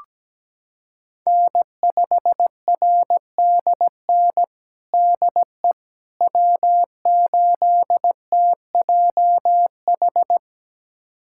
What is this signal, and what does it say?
Morse code, heard as a keyed tone, about 17 wpm, 700 Hz. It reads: N5RDN DE W8TJH
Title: USA Callsign DE Callsign - 5 Characters - 17wpm